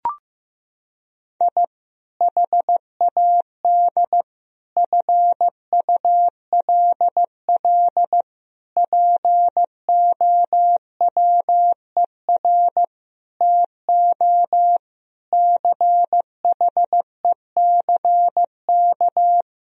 Morse code, heard as a keyed tone, about 15 wpm, 700 Hz. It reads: I HAD FULL POWER TO CHECK
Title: Sentences from Top 1000 Words - Review Part 3 - 15wpm